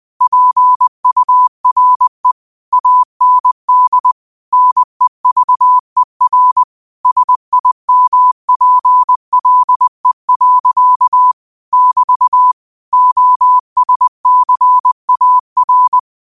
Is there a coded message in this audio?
Yes: Morse code from 0.2 to 16.0 s, heard as a keyed tone, about 20 wpm, 1000 Hz. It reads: PURE AND NEVER SIMPLE. <BT> OSCAR